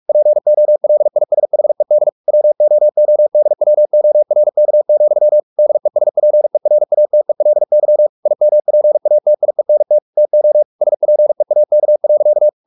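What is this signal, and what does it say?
Morse code, heard as a keyed tone, about 44 wpm, 600 Hz. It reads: POLISHED WOODWORK, DESPERATELY IMPATIENT TO SPEAK.
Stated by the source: Morse practice at 45 wpm, the novel Never by Ken Follett in English, from CW med Gnister.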